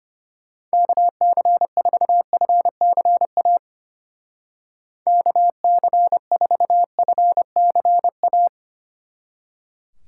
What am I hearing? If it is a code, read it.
Morse code, heard as a keyed tone, about 33 wpm, 700 Hz. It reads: KC4FCA KC4FCA